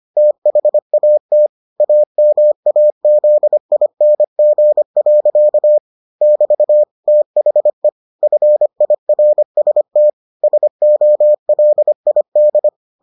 Morse code, heard as a keyed tone, about 25 wpm, 600 Hz. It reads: THAT AMAZING. <BT> THE FIRST SOLID